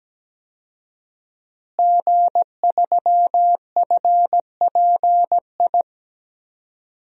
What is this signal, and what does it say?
Morse code, heard as a keyed tone, about 17 wpm, 700 Hz. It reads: G3FPI
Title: DXCC-100 Call Signs - 5 Characters - Repeated 2x - 17wpm